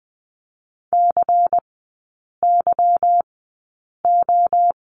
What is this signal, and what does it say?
Morse code, heard as a keyed tone, about 20 wpm, spaced as for 8 wpm, 700 Hz. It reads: CYO
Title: 152 Lesson 25e - Letter p - Three Characters 20wpm